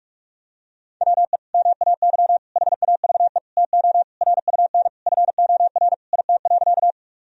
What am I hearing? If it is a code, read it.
Morse code, heard as a keyed tone, about 46 wpm, 700 Hz. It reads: WE MAY HAVE TO RUN FOR IT.
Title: Sentences from Top 1000 Words - Review Part 1 - 45wpm